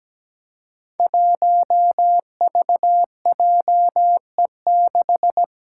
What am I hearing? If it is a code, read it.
Morse code, heard as a keyed tone, about 17 wpm, 700 Hz. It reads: 1VJE6
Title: Five Challenging Letter-Number Combinations 17wpm